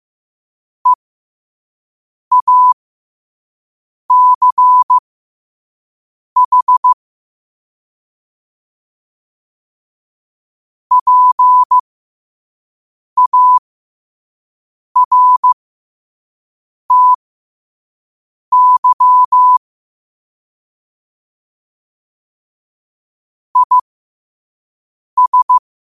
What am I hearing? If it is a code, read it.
Morse code, heard as a keyed tone, about 15 wpm, spaced as for 5 wpm, 1000 Hz. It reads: EACH PARTY IS
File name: qotdfiles / 5wpm